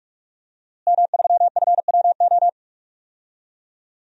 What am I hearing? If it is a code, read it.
Morse code, heard as a keyed tone, about 42 wpm, 700 Hz. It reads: M3FWO
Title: DXCC-100 Call Signs - 5 Characters - Repeated 3x - 45wpm